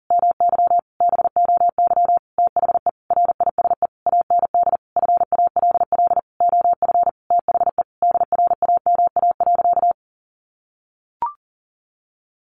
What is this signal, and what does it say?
Morse code, heard as a keyed tone, about 40 wpm, 700 Hz. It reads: MY BOY THE RISE AND FALL OF THE DRAMA.
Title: Sentences from Top 1500 Words 40wpm